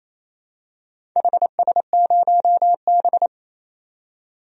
Morse code, heard as a keyed tone, about 28 wpm, 700 Hz. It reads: HS0B
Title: DXCC-100 Call Signs - 4 Characters - 28wpm